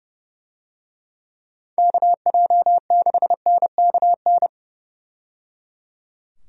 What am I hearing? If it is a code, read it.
Morse code, heard as a keyed tone, about 30 wpm, 700 Hz. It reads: KJ6NKN
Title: USA Call Signs - 6 Characters - 30wpm